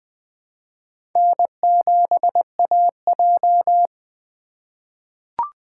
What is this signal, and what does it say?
Morse code, heard as a keyed tone, about 20 wpm, 700 Hz. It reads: N7AJ
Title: USA Call Signs - 4x Character Spacing - 20wpm